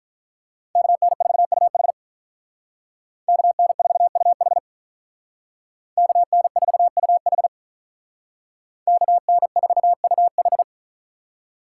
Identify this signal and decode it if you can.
Morse code, heard as a keyed tone, about 44 wpm, 700 Hz. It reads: KN4UH KN4UH KN4UH KN4UH